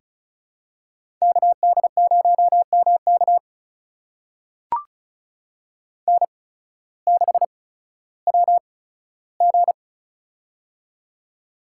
Morse code, heard as a keyed tone, about 36 wpm, spaced as for 16 wpm, 700 Hz. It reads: KD0MK N6WG
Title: USA Call Signs - 8x Character Spacing - 35wpm